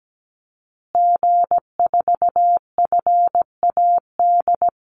Morse code, heard as a keyed tone, about 17 wpm, 700 Hz. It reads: G4FAD
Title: Call Signs Repeated 2x - 17wpm